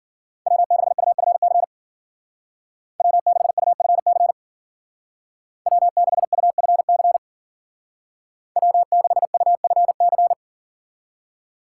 Morse code, heard as a keyed tone, about 53 wpm, 700 Hz. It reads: W6UFC W6UFC W6UFC W6UFC